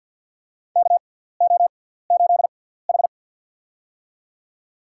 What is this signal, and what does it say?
Morse code, heard as a keyed tone, about 53 wpm, spaced as for 17 wpm, 700 Hz. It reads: KO8H
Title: USA Call Signs - 6x Character Spacing - 50wpm